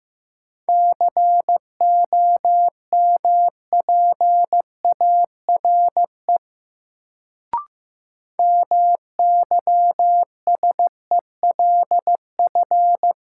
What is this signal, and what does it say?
Morse code, heard as a keyed tone, about 16 wpm, 700 Hz. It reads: COMPARE MYSELF
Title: Sentences from Top 1300 Words - New Word Drill Alt - 15wpm